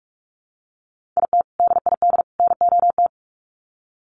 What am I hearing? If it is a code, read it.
Morse code, heard as a keyed tone, about 44 wpm, 700 Hz. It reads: IT DID NOT
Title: Sets of 3 Words 45wpm